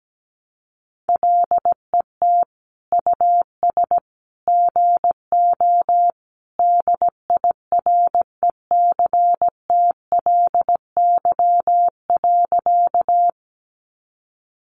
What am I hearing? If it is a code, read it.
Morse code, heard as a keyed tone, about 17 wpm, 700 Hz. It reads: LET US GO DIRECTLY.
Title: Sentences from Top 800 Words 17wpm